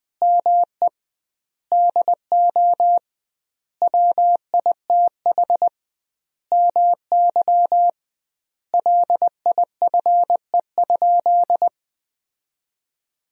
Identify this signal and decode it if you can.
Morse code, heard as a keyed tone, about 20 wpm, 700 Hz. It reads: ME DO WITH MY LIFE?
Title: 2x Word-Spacing - Sentences from Top 300 Words - 20wpm